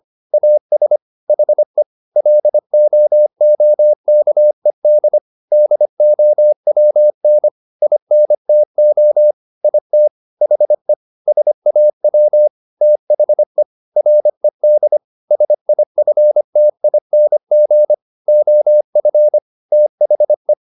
Morse code, heard as a keyed tone, about 25 wpm, 600 Hz. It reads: AS HE LOOKED DOWN INTO IT HE SAW THE RED SIFTING OF THE
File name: the old man and the sea part 1 25wpm